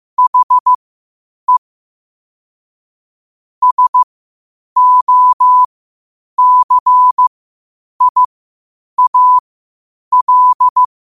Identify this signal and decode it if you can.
Morse code, heard as a keyed tone, about 15 wpm, spaced as for 8 wpm, 1000 Hz. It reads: HE SOCIAL